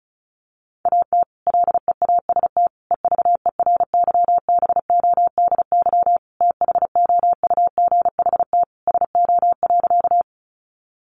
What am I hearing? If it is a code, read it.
Morse code, heard as a keyed tone, about 34 wpm, 700 Hz. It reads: AT LEAST EVERYBODY THOUGHT SO.
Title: Sentences from Top 1400 Words 35wpm